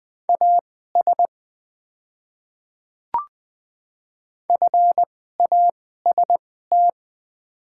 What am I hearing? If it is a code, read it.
Morse code, heard as a keyed tone, about 20 wpm, spaced as for 15 wpm, 690 Hz. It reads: AS FAST
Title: 2x Character Spacing - Top 500 Words - 20wpm